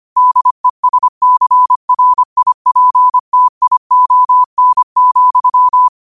Morse code, heard as a keyed tone, about 25 wpm, 1000 Hz. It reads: DESCRIPTION,
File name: qotdfiles / 25wpm